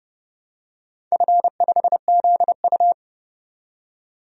Morse code, heard as a keyed tone, about 30 wpm, 700 Hz. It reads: F5ZU
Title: DXCC-100 Call Signs - 4 Characters - 30wpm